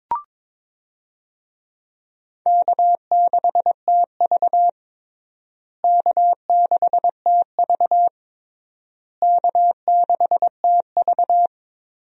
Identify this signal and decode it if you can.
Morse code, heard as a keyed tone, about 22 wpm, 700 Hz. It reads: K6TV K6TV K6TV
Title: USA Call Signs - 4 Characters - Repeated 3x - 22wpm